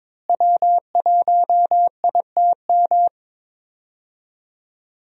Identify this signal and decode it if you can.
Morse code, heard as a keyed tone, about 22 wpm, 700 Hz. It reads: W1ITM
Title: USA Call Signs - 5 Characters - Repeated 2x - 22wpm